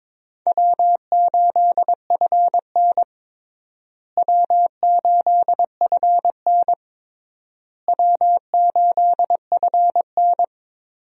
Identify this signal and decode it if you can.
Morse code, heard as a keyed tone, about 22 wpm, 700 Hz. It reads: W8FN W8FN W8FN
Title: USA Call Signs - 4 Characters - Repeated 3x - 22wpm